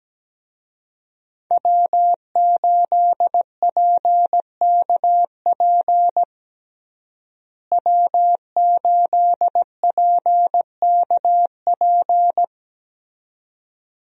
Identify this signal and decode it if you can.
Morse code, heard as a keyed tone, about 17 wpm, 700 Hz. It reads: W8PKP W8PKP
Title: USA Call Signs - 5 Characters - Repeated 2x - 17wpm